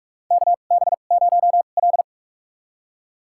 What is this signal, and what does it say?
Morse code, heard as a keyed tone, about 44 wpm, spaced as for 32 wpm, 700 Hz. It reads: KB0L
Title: USA Call Signs - 4 Characters - 2x Character Spacing - 45wpm